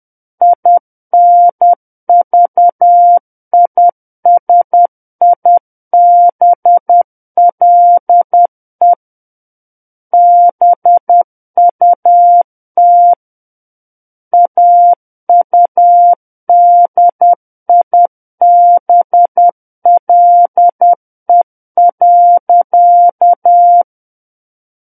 Morse code, heard as a keyed tone, about 10 wpm, 700 Hz. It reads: INVISIBLE BUT AUDIBLE.
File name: Book One / 01-Chapter 1 - 10 WPM